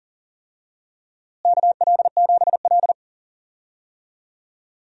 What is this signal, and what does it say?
Morse code, heard as a keyed tone, about 40 wpm, 700 Hz. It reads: KL7L
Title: DXCC-100 Call Signs - 4 Characters - Repeated 3x - 40wpm